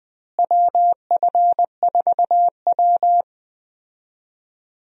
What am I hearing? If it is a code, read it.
Morse code, heard as a keyed tone, about 20 wpm, 700 Hz. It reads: WF4W
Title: USA Call Signs - 4 Characters - Repeated 2x - 20wpm